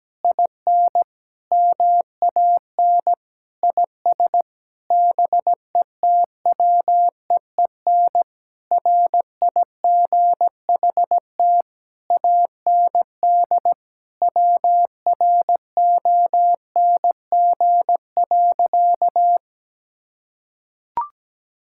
Morse code, heard as a keyed tone, about 17 wpm, 700 Hz. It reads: IN MAN IS BETWEEN RIGHT AND WRONG.
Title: Sentences from Top 1600 Words 17wpm